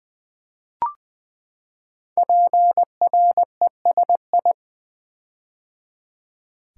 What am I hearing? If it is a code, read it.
Morse code, heard as a keyed tone, about 20 wpm, 700 Hz. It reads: PRESI
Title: Build A Word - 5 Letters At A Time - 12-Letter Words - 20wpm